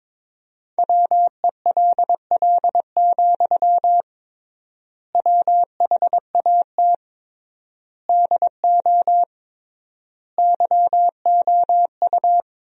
Morse code, heard as a keyed tone, about 22 wpm, 700 Hz. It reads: WELL, WHAT DO YOU
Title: 3x Word-Spacing - Sentences from Top 200 Words - 22wpm